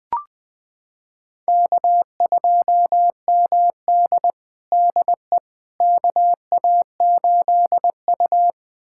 Morse code, heard as a keyed tone, about 20 wpm, 700 Hz. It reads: K2MD DE KA8U
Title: USA Callsign DE Callsign - 4 Characters - 20wpm